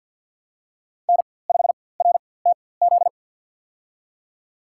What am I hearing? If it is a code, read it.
Morse code, heard as a keyed tone, about 52 wpm, spaced as for 23 wpm, 700 Hz. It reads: N5RTZ
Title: USA Call Signs - 5 Characters - 4x Character Spacing - 50wpm